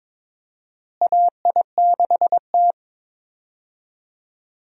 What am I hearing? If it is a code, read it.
Morse code, heard as a keyed tone, about 21 wpm, 700 Hz. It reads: AI6T
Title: USA Call Signs - 4 Characters - Repeated 3x - 22wpm